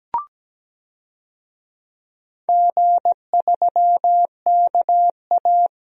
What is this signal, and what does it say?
Morse code, heard as a keyed tone, about 17 wpm, 700 Hz. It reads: G3KA
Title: DXCC-100 Call Signs - 4 Characters - 17wpm